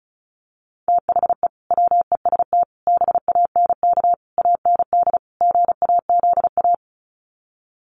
Morse code, heard as a keyed tone, about 34 wpm, 700 Hz. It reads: THE WEST BANK AND GAZA
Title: Sets of 5 Words 35wpm